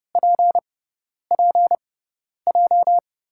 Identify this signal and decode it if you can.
Morse code, heard as a keyed tone, about 32 wpm, spaced as for 10 wpm, 700 Hz. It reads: PPJ
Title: USA Call Signs - 6 Characters - 6x Character Spacing - 30wpm